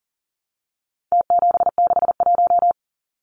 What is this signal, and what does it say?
Morse code, heard as a keyed tone, about 40 wpm, 700 Hz. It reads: T761
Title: Numbers - Contest Style - 4-Digit 40wpm